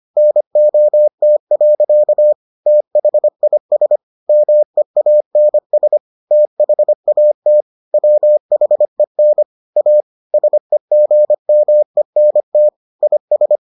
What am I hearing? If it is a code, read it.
Morse code, heard as a keyed tone, about 25 wpm, 600 Hz. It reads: NOT. THIS MEANS THAT WHEN A SEGMENT IS